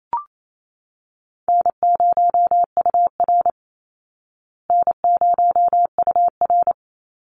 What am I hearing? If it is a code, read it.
Morse code, heard as a keyed tone, about 28 wpm, 690 Hz. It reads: N0UR N0UR
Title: Call Signs Repeated 3x - 28wpm